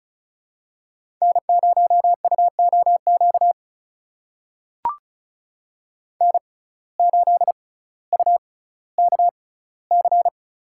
Morse code, heard as a keyed tone, about 36 wpm, spaced as for 18 wpm, 700 Hz. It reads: N0UOQ N8UKC
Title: USA Call Signs - 5 Characters - 6x Character Spacing - 35wpm